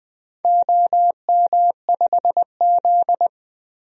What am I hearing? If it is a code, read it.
Morse code, heard as a keyed tone, about 20 wpm, 700 Hz. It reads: OM5Z
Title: DXCC-100 Call Signs - 4 Characters - 20wpm